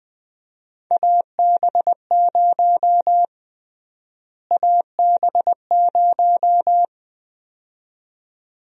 Morse code, heard as a keyed tone, about 20 wpm, 700 Hz. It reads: AB0 AB0
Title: USA Call Sign Prefixes - Repeated 2x - 20wpm